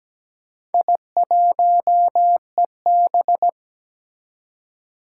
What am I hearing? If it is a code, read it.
Morse code, heard as a keyed tone, about 17 wpm, 700 Hz. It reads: I1EB
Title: Four Challenging Letter-Number Combinations 17wpm